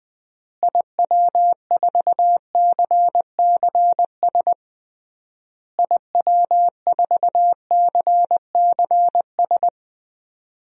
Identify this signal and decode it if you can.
Morse code, heard as a keyed tone, about 20 wpm, 700 Hz. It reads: IW4CCS IW4CCS
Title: DXCC-100 Call Signs - 6 Characters - Repeated 3x - 20wpm